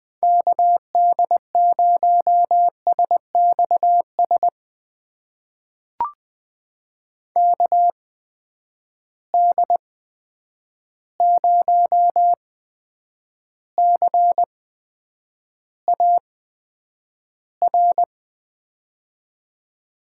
Morse code, heard as a keyed tone, about 20 wpm, spaced as for 9 wpm, 700 Hz. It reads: KD0SXS KD0CAR
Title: USA Call Signs - 6 Characters - 8x Character Spacing - 20wpm